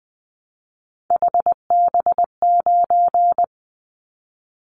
Morse code, heard as a keyed tone, about 20 wpm, 700 Hz. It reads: HB9
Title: Callsign Prefixes for Top 25 DXCC - 20wpm